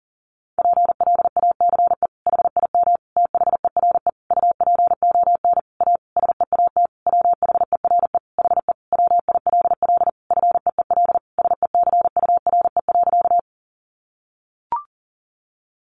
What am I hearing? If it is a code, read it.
Morse code, heard as a keyed tone, about 37 wpm, 700 Hz. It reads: PLACE HIM THERE UPON A SEAT WHERE HE WILL FEEL SECURE.
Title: Sentences from Top 1500 Words 40wpm